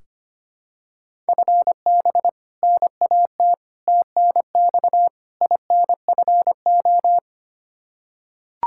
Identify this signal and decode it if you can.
Morse code, heard as a keyed tone, about 25 wpm, 700 Hz. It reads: FB NAT TNX INFO
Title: Practice QSO - Exchange 3 - More Info - 25wpm